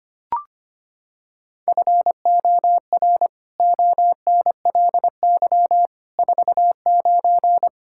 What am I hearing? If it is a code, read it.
Morse code, heard as a keyed tone, about 25 wpm, 700 Hz. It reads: FOR ONLY 49
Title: Pangram Challenge v2 - 25wpm